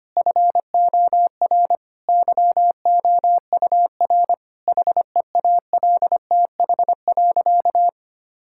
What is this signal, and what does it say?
Morse code, heard as a keyed tone, about 25 wpm, 700 Hz. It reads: FOR YOUR HEALTH.
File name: practice / 25wpm / Difficult Sentences with Hints - 25wpm